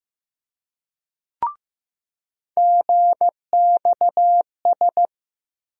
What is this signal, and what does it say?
Morse code, heard as a keyed tone, about 15 wpm, 700 Hz. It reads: GXS
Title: USA Call Sign Postfixes - 3 Letters 15wpm